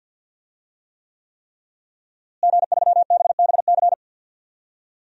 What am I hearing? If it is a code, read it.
Morse code, heard as a keyed tone, about 52 wpm, 700 Hz. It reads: G3BBC
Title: DXCC-100 Call Signs - 5 Characters - Repeated 3x - 50wpm